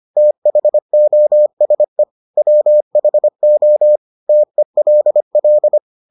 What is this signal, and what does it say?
Morse code, heard as a keyed tone, about 25 wpm, 600 Hz. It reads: THOSE WHO TELL